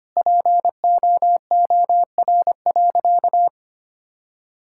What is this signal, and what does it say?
Morse code, heard as a keyed tone, about 25 wpm, 700 Hz. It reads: POOR.
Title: Sentences from Top 1000 Words - Review Part 6 - 25wpm